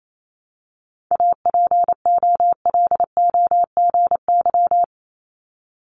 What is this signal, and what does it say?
Morse code, heard as a keyed tone, about 28 wpm, 700 Hz. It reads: APOLOGY
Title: Multi-letter Phonemes 28wpm